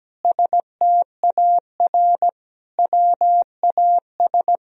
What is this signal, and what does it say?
Morse code, heard as a keyed tone, about 17 wpm, 700 Hz. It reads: STAR WAS
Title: Sentences from Top 1000 Words - Review Part 4 - 17wpm